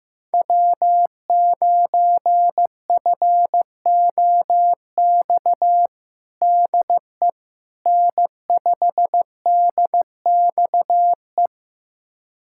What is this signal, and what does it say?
Morse code, heard as a keyed tone, about 15 wpm, 700 Hz. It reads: W9FOX DE N5DXE
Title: USA Callsign DE Callsign - 5 Characters - 15wpm